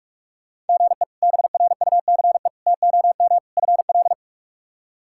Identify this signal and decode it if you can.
Morse code, heard as a keyed tone, about 46 wpm, 700 Hz. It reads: GE BRUCE TOM FL
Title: K1USN SST QSO Practice - 45wpm